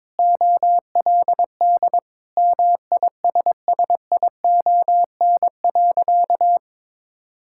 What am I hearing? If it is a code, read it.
Morse code, heard as a keyed tone, about 22 wpm, 700 Hz. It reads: OLD MISSION.
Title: Sentences from Top 1600 Words 22wpm